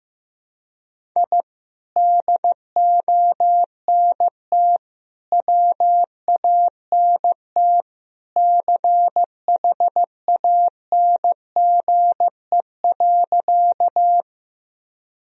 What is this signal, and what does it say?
Morse code, heard as a keyed tone, about 16 wpm, 700 Hz. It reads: I DONT WANT CHANGE.
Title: Sentences from Top 1000 Words - Review Part 1 - 15wpm